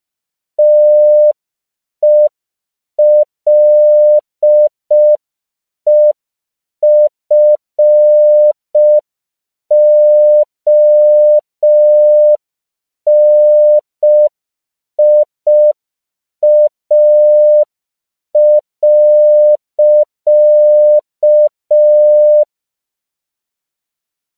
Morse code, heard as a keyed tone, about 5 wpm, 600 Hz. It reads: TELEFONIA.
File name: CW / 5PPM